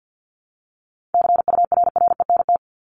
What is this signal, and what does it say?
Morse code, heard as a keyed tone, about 50 wpm, 700 Hz. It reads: CURRENT
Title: Top 501-750 Words 50wpm